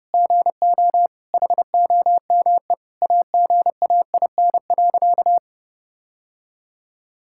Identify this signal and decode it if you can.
Morse code, heard as a keyed tone, about 30 wpm, 700 Hz. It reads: GO HOME AGAIN.
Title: Sentences from Top 1000 Words - Review Part 6 - 30wpm